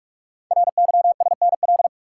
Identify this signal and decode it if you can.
Morse code, heard as a keyed tone, about 44 wpm, 700 Hz. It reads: RYSNL
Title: Code Groups - 5 Letters - 45wpm